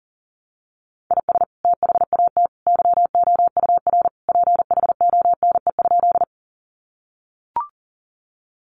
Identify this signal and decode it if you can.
Morse code, heard as a keyed tone, about 40 wpm, 700 Hz. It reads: IS THAT YOUR PHONE?
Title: Sentences from Top 1100 Words 40wpm